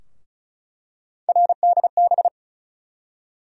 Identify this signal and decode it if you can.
Morse code, heard as a keyed tone, about 36 wpm, 700 Hz. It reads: RDB
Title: USA Call Sign Postfixes - 3 Letters 35wpm